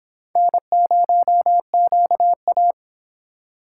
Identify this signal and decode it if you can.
Morse code, heard as a keyed tone, about 30 wpm, 700 Hz. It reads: N0QA